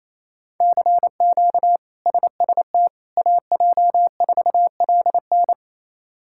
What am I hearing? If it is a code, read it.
Morse code, heard as a keyed tone, about 28 wpm, 700 Hz. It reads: CQ SST AJ4LN
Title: K1USN SST QSO Practice - 28wpm